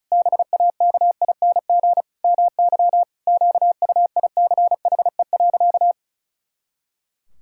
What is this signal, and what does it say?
Morse code, heard as a keyed tone, about 34 wpm, 700 Hz. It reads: BAKING MY QUICHE.
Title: Pangram Challenge v2 - 35wpm